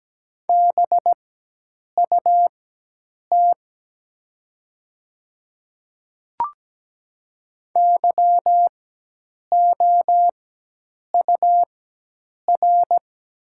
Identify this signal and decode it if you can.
Morse code, heard as a keyed tone, about 17 wpm, spaced as for 8 wpm, 700 Hz. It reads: BUT YOUR
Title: 4x Character Spacing - Top 500 Words - 17wpm